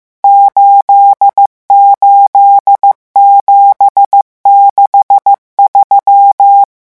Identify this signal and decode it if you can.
Morse code, heard as a keyed tone, about 15 wpm, 800 Hz. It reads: 88763
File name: morse mp3 / revision11 chiffres vitesse 14